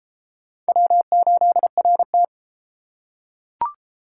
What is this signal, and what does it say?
Morse code, heard as a keyed tone, about 31 wpm, 700 Hz. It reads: W8RT